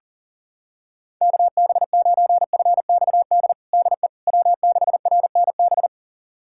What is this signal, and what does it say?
Morse code, heard as a keyed tone, about 37 wpm, 700 Hz. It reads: KB9FXD DE W6RNB
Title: USA Callsign DE Callsign 40wpm